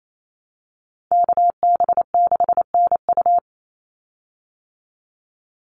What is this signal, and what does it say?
Morse code, heard as a keyed tone, about 28 wpm, 700 Hz. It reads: KB6NU
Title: Call Signs Repeated 2x - 28wpm